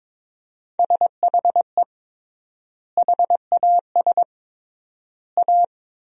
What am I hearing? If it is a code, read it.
Morse code, heard as a keyed tone, about 22 wpm, 700 Hz. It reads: SHE HAS A
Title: 3x Word-Spacing - Sentences from Top 200 Words - 22wpm